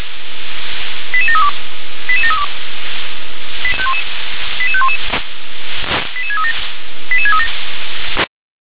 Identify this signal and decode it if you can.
Five-tone selective call (ZVEI), heard as quick runs of five tones, tones of about 70 ms. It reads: 88732 88732 88511 88637 88637
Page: /357/SDR/ZVEI/88500/